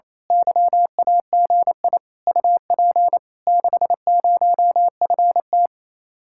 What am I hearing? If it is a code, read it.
Morse code, heard as a keyed tone, about 26 wpm, 700 Hz. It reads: YAGI UP 60FT
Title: Speed-Racing Common QSO Phrases - 28wpm